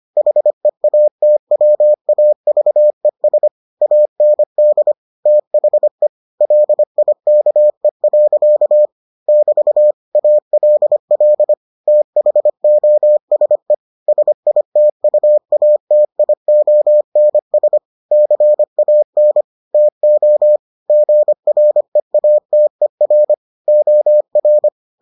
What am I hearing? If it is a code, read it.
Morse code, heard as a keyed tone, about 25 wpm, 600 Hz. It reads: HEATWAVES AND THE LIKE. <BT> ALL THOSE SITUATIONS CAN TO GREATER OR